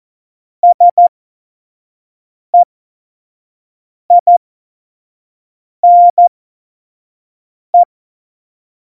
Morse code, heard as a keyed tone, about 14 wpm, spaced as for 5 wpm, 700 Hz. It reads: SEINE
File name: Morsekurs / Witze / Witz 03. November 2020, 25 F70